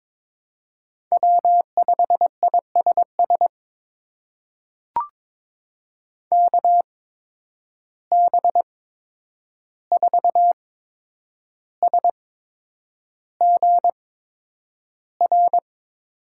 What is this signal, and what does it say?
Morse code, heard as a keyed tone, about 22 wpm, spaced as for 9 wpm, 700 Hz. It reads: W5ISS KB4SGR
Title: USA Call Signs - 8x Character Spacing - 22wpm